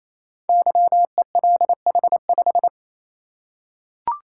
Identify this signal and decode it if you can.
Morse code, heard as a keyed tone, about 27 wpm, 700 Hz. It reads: YELH5
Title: Code Groups - 5 Letter:Numbers - 28wpm